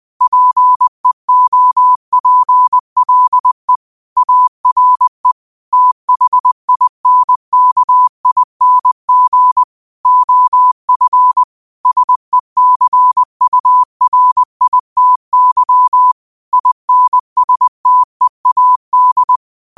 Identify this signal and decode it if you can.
Morse code, heard as a keyed tone, about 20 wpm, 1000 Hz. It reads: PEOPLE ARE THINKING OF SECURITY INSTEAD
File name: qotdfiles / 20wpm